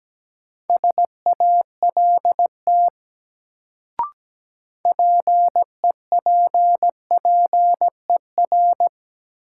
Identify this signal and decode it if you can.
Morse code, heard as a keyed tone, about 17 wpm, 700 Hz. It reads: SALT PEPPER
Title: Sets of 5 Words - New Word Drill - 17wpm